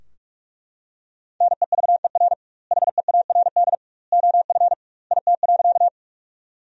Morse code, heard as a keyed tone, about 42 wpm, 700 Hz. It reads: NEVER HEARD OF IT.
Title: 2x Word-Spacing - Sentences from Top 400 Words - 45wpm